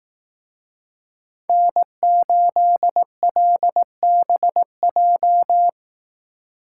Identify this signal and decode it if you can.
Morse code, heard as a keyed tone, about 18 wpm, 700 Hz. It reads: N8LBJ
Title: Speed Racing - USA 1x3 Call Signs 15wpm